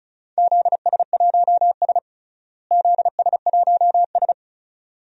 Morse code, heard as a keyed tone, about 34 wpm, 700 Hz. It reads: ZS1S ZS1S
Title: DXCC-100 Call Signs - 4 Characters - Repeated 2x - 35wpm